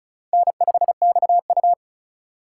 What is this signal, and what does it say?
Morse code, heard as a keyed tone, about 36 wpm, 700 Hz. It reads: N5XU
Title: USA Call Signs - 4 Characters - 3x Character Spacing - 35wpm